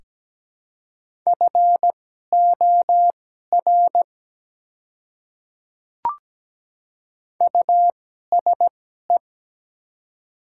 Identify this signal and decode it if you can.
Morse code, heard as a keyed tone, about 17 wpm, spaced as for 12 wpm, 700 Hz. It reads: FOR USE
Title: 2x Character Spacing - Top 200 Words - 17wpm